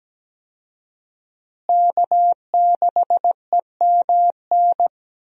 Morse code, heard as a keyed tone, about 17 wpm, 700 Hz. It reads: K6EMN
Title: USA Call Signs - 5 Characters - Repeated 2x - 17wpm